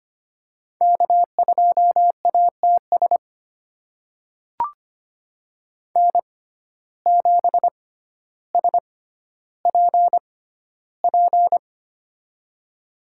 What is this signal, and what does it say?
Morse code, heard as a keyed tone, about 25 wpm, spaced as for 13 wpm, 700 Hz. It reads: K2ATS N7SPP
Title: USA Call Signs - 5 Characters - 6x Character Spacing - 25wpm